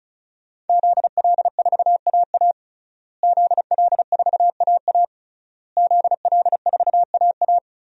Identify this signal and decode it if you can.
Morse code, heard as a keyed tone, about 36 wpm, 700 Hz. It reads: ZL4AA ZL4AA ZL4AA
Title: DXCC-100 Call Signs - 5 Characters - Repeated 3x - 35wpm